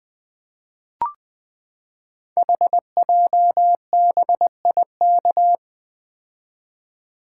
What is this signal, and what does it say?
Morse code, heard as a keyed tone, about 21 wpm, 700 Hz. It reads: HJBIK
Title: Code Groups - 5 Letters - 20wpm